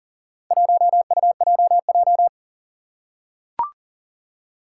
Morse code, heard as a keyed tone, about 40 wpm, 700 Hz. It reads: 1UJJ
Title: Four Challenging Letter-Number Combinations 40wpm